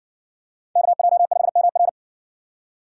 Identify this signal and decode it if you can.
Morse code, heard as a keyed tone, about 60 wpm, 700 Hz. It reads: KJ5MR